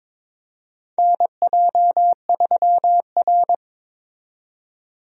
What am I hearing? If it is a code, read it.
Morse code, heard as a keyed tone, about 22 wpm, 700 Hz. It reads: NJ3R
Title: USA Call Signs - 4 Characters - 22wpm